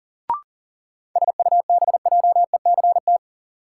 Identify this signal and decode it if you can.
Morse code, heard as a keyed tone, about 40 wpm, 700 Hz. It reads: SUBJECT
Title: Sentences from Top 800 Words - New Word Drill - 40wpm